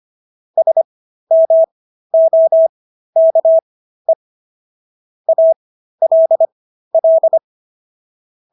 Morse code, heard as a keyed tone, about 25 wpm, spaced as for 13 wpm, 650 Hz. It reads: SMOKE ALL
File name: SS106 13